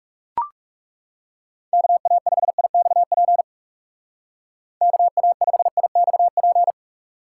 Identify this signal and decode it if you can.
Morse code, heard as a keyed tone, about 36 wpm, 700 Hz. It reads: KA5IXP KA5IXP
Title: Speed Racing - USA 2x3 Call Signs 30wpm